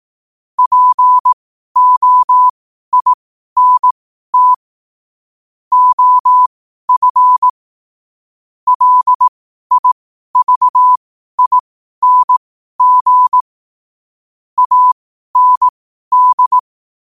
Morse code, heard as a keyed tone, about 18 wpm, spaced as for 12 wpm, 1000 Hz. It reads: POINT OF LIVING AND
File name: qotdfiles / 12wpm